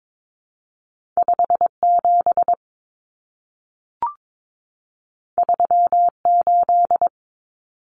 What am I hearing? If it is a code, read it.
Morse code, heard as a keyed tone, about 22 wpm, 700 Hz. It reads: 57 38
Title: Common Cut Numbers - 2-Digit 22wpm